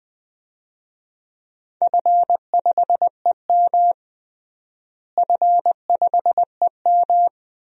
Morse code, heard as a keyed tone, about 20 wpm, 700 Hz. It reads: F5EM F5EM
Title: DXCC-100 Call Signs - 4 Characters - Repeated 3x - 20wpm